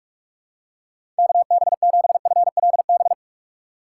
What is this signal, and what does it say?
Morse code, heard as a keyed tone, about 46 wpm, 700 Hz. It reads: KB7FLB
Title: USA Call Signs - 6 Characters - 3x Character Spacing - 45wpm